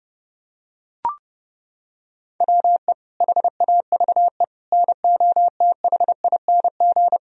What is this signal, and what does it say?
Morse code, heard as a keyed tone, about 30 wpm, 700 Hz. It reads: WE HAVE NOTHING
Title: Sentences from Top 1000 Words - Review Part 1 - 30wpm